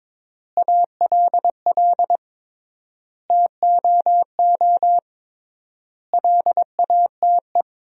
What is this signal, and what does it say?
Morse code, heard as a keyed tone, about 22 wpm, 700 Hz. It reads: ALL TOO LATE
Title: 3x Word-Spacing - Sentences from Top 300 Words - 22wpm